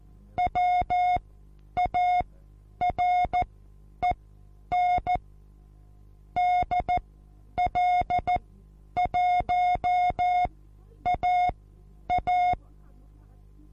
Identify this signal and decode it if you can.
Morse code, heard as a keyed tone, about 14 wpm, spaced as for 9 wpm, 720 Hz. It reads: WAREN DL1AA
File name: Onlinelehrgang / morsen / DJ4UF Morsen23